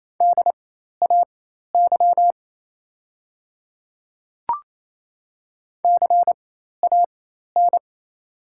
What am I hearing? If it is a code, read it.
Morse code, heard as a keyed tone, about 28 wpm, spaced as for 13 wpm, 690 Hz. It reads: DAY CAN
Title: 4x Character Spacing - Top 100 Words - 28wpm